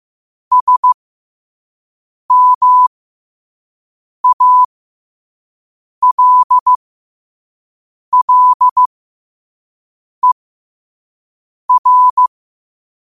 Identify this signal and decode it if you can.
Morse code, heard as a keyed tone, about 15 wpm, spaced as for 5 wpm, 1000 Hz. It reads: SMALLER
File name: qotdfiles / 5wpm